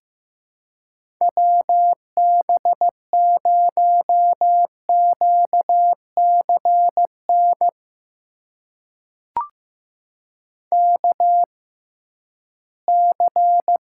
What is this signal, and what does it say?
Morse code, heard as a keyed tone, about 15 wpm, spaced as for 11 wpm, 700 Hz. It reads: WB0QCN KC
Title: USA Call Signs - 6 Characters - 6x Character Spacing - 15wpm